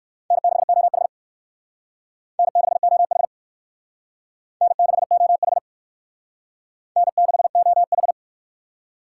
Morse code, heard as a keyed tone, about 59 wpm, 700 Hz. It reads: N6OH N6OH N6OH N6OH